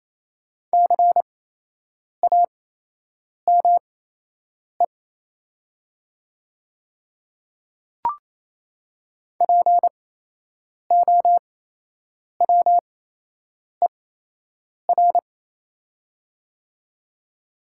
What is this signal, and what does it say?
Morse code, heard as a keyed tone, about 28 wpm, spaced as for 8 wpm, 700 Hz. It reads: CAME POWER
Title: 8x Character Spacing - Top 500 Words - 28wpm